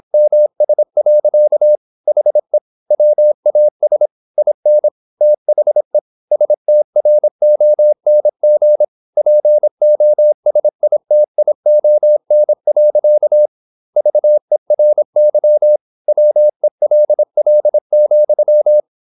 Morse code, heard as a keyed tone, about 26 wpm, 600 Hz. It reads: MS. HE WAS IN THE STRONG POSITION. VERY WELL,
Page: Never 26wpm | CW med Gnister